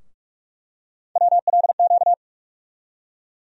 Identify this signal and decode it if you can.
Morse code, heard as a keyed tone, about 46 wpm, 700 Hz. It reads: WLQ
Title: USA Call Sign Postfixes - 3 Letters 45wpm